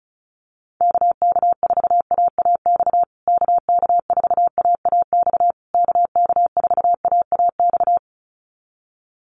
Morse code, heard as a keyed tone, about 34 wpm, 700 Hz. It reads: KK4AAX KK4AAX KK4AAX
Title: USA Call Signs Repeated 3x 35wpm